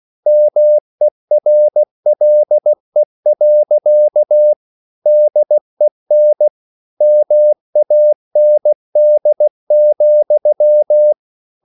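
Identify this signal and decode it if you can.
Morse code, heard as a keyed tone, about 16 wpm, 600 Hz. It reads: MERLE. DEN MAND,
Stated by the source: Slow (QRS) Morse code at 16 wpm, from CW med Gnister.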